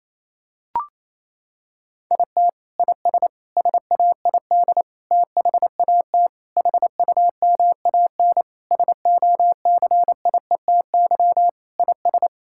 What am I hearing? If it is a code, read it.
Morse code, heard as a keyed tone, about 28 wpm, 700 Hz. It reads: IT IS SAID THAT HUMAN SOCIETY IS
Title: Sentences from Top 1000 Words - Review Part 6 - 28wpm